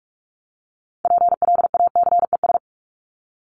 Morse code, heard as a keyed tone, about 44 wpm, 700 Hz. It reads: PLACES
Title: ES:S Suffix - Top 100 Words with Base Words 45wpm